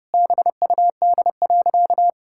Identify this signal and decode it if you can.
Morse code, heard as a keyed tone, about 30 wpm, 700 Hz. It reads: BUD.
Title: Pangram Challenge v2 - 30wpm